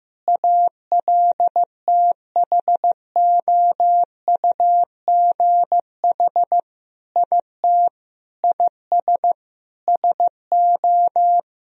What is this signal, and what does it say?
Morse code, heard as a keyed tone, about 15 wpm, 700 Hz. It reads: ALTHOUGH IT IS SO
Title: Sentences from Top 1000 Words - Review Part 3 - 15wpm